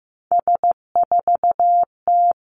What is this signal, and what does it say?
Morse code, heard as a keyed tone, about 15 wpm, 700 Hz. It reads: S4T
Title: Three Challenging Letter-Number Combinations 15wpm